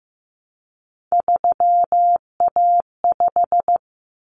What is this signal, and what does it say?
Morse code, heard as a keyed tone, about 15 wpm, 700 Hz. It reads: 3A5
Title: Common Cut Numbers - 3-Digit 15wpm